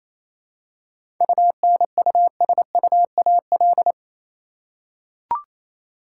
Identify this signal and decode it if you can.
Morse code, heard as a keyed tone, about 28 wpm, 700 Hz. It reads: UNUSUAL
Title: Seven Letter Words - 28wpm